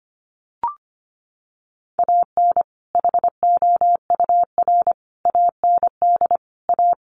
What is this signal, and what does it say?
Morse code, heard as a keyed tone, about 25 wpm, 700 Hz. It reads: AN HOUR AND A
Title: Sentences from Top 400 Words 25wpm